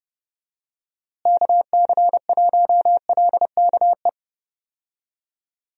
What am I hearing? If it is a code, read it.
Morse code, heard as a keyed tone, about 30 wpm, 700 Hz. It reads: KC1LKE
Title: USA Call Signs - 6 Characters - Repeated 3x - 30wpm